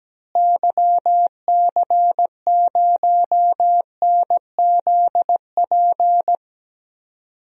Morse code, heard as a keyed tone, about 17 wpm, 700 Hz. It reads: YC0NZP
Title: DXCC-100 Call Signs - 6 Characters - Repeated 2x - 17wpm